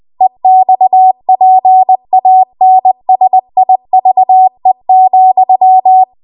Morse code, heard as a keyed tone, about 20 wpm, 750 Hz. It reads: EXPANSIVE,
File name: Morse / 20WPM